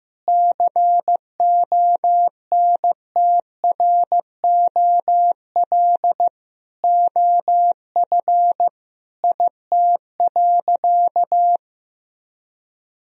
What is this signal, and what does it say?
Morse code, heard as a keyed tone, about 16 wpm, 700 Hz. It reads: CONTROL OF IT.
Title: Sentences from Top 1000 Words - Review Part 3 - 15wpm